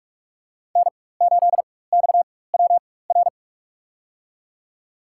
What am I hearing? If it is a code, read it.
Morse code, heard as a keyed tone, about 46 wpm, spaced as for 21 wpm, 700 Hz. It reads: N8XWR
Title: USA Call Signs - 5 Characters - 4x Character Spacing - 45wpm